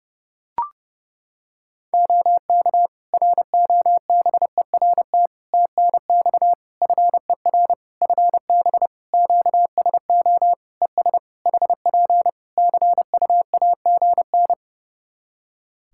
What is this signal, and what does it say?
Morse code, heard as a keyed tone, about 32 wpm, 700 Hz. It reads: OK ROBERT TNX FER FB QSO ES HP CUAGN
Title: K1USN SST QSO Practice 30wpm